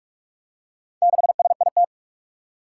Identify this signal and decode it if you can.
Morse code, heard as a keyed tone, about 44 wpm, 700 Hz. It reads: 6SIT
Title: Four Challenging Letter-Number Combinations 45wpm